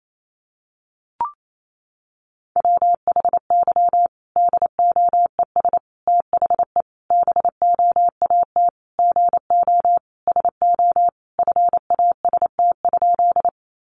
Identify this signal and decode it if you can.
Morse code, heard as a keyed tone, about 28 wpm, 700 Hz. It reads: WHY DOES THE BOAT GO SO FAST?
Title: Sentences from Top 500 Words 28wpm